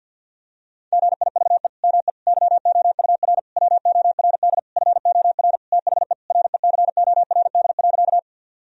Morse code, heard as a keyed tone, about 50 wpm, 700 Hz. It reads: GIVE ME YOUR WORD FOR THE RECORD.
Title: Sentences from Top 1000 Words - Review Part 3 - 50wpm